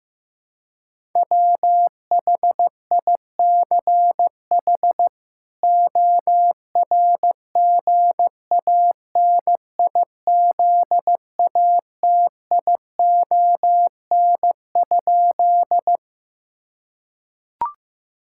Morse code, heard as a keyed tone, about 15 wpm, 700 Hz. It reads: WHICH ORGANIZATION?
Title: Sentences from Top 1000 Words - Review Part 5 - 15wpm